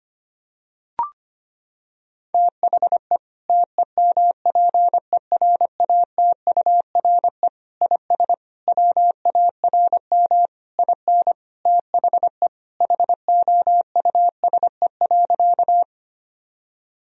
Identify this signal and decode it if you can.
Morse code, heard as a keyed tone, about 25 wpm, 700 Hz. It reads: THE TEMPERATURE IS WARM IN THE HOUSE.
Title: Sentences from Top 1000 Words - Review Part 4 - 25wpm